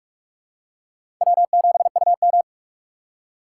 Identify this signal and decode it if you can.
Morse code, heard as a keyed tone, about 41 wpm, 690 Hz. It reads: W7UM
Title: USA Call Signs - 4 Characters - 3x Character Spacing - 45wpm